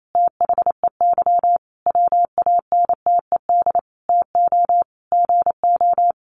Morse code, heard as a keyed tone, about 28 wpm, 700 Hz. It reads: THEY WANTED TO GO
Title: A Sound Of Thunder 28wpm